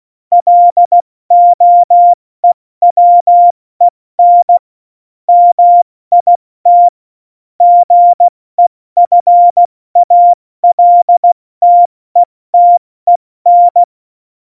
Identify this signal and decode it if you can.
Morse code, heard as a keyed tone, about 16 wpm, 700 Hz. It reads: LOEWEN MIT GEFALTETEN
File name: Morsekurs / Witze / Witz 09. Oktober 2018, 70 F80